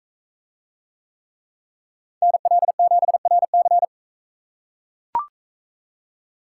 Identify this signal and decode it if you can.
Morse code, heard as a keyed tone, about 39 wpm, 700 Hz. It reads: NL7RC